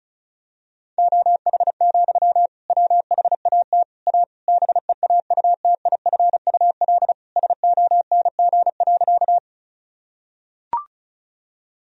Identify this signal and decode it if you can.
Morse code, heard as a keyed tone, about 34 wpm, 700 Hz. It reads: OH, WHAT A BEAUTIFUL SONG.
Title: Sentences from Top 1000 Words - Review Part 2 - 35wpm